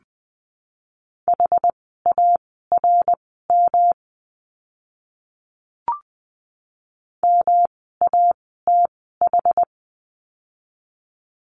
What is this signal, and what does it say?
Morse code, heard as a keyed tone, about 20 wpm, spaced as for 14 wpm, 700 Hz. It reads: HARM MATH
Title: 108 Lesson 18f - Letter m - Words 20wpm